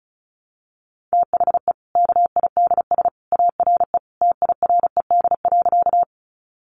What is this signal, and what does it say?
Morse code, heard as a keyed tone, about 34 wpm, 700 Hz. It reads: THE KIDS ARE TIRED.
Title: Easy Phrases 35wpm